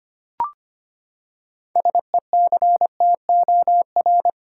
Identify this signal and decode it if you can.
Morse code, heard as a keyed tone, about 25 wpm, 700 Hz. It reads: SECTOR
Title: Six Letter Words - 25wpm